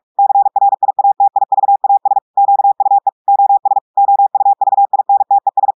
Morse code, heard as a keyed tone, about 46 wpm, 800 Hz. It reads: CRIATIVAS QUE OS OUVINTES
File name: audios / portugais / portugais texte en clair 2 vitesse 45 mots minutes